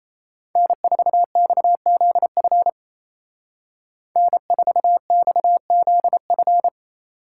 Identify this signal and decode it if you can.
Morse code, heard as a keyed tone, about 36 wpm, 700 Hz. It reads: N4XZF N4XZF